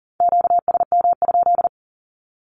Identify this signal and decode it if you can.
Morse code, heard as a keyed tone, about 40 wpm, 700 Hz. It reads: QSM?
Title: Q-Codes 40wpm